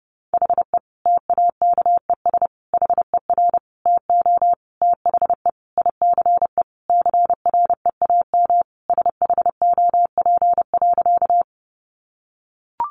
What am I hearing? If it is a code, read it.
Morse code, heard as a keyed tone, about 30 wpm, 700 Hz. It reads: HE TAKES HER TO THE ICE CREAM SHOP.
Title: Easy Phrases 30wpm